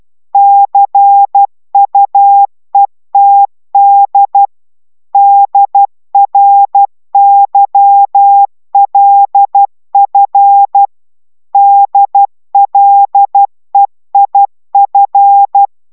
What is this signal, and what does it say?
Morse code, heard as a keyed tone, about 12 wpm, 800 Hz. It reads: CUETD DRYLF DLEIF